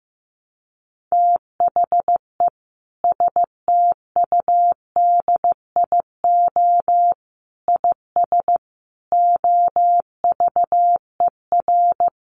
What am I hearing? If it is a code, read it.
Morse code, heard as a keyed tone, about 15 wpm, 700 Hz. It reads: THE STUDIO IS OVER
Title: Sentences from Top 1200 Words 15wpm